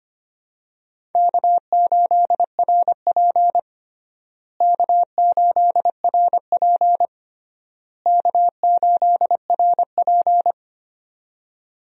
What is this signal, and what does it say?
Morse code, heard as a keyed tone, about 25 wpm, 700 Hz. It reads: K8RP K8RP K8RP
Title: USA Call Signs - 4 Characters - Repeated 3x - 25wpm